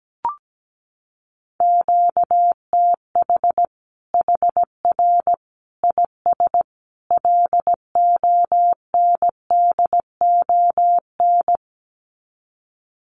Morse code, heard as a keyed tone, about 17 wpm, 700 Hz. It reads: QTH HR IS LONDON
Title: Common QSO Phrases 17wpm